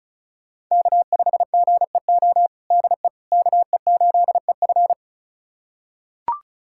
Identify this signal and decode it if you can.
Morse code, heard as a keyed tone, about 34 wpm, 700 Hz. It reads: K5GEO DE KE8EF
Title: USA Callsign DE Callsign - 5 Characters - 35wpm